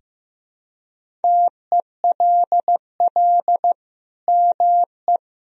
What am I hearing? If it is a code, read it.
Morse code, heard as a keyed tone, about 15 wpm, 700 Hz. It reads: TELL ME
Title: Sentences from Top 1000 Words - Review Part 3 - 15wpm